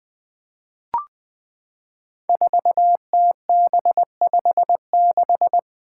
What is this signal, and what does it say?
Morse code, heard as a keyed tone, about 20 wpm, 700 Hz. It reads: 4TB56
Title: Five Challenging Letter-Number Combinations 20wpm